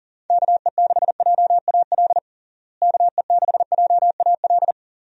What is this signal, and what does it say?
Morse code, heard as a keyed tone, about 40 wpm, 700 Hz. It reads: KE6JAL KE6JAL
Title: USA Call Signs - 6 Characters - Repeated 2x - 40wpm